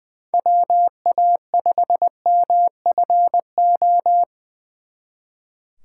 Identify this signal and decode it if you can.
Morse code, heard as a keyed tone, about 23 wpm, 700 Hz. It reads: WA5MFO
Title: Speed Racing - USA 2x3 Call Signs 20wpm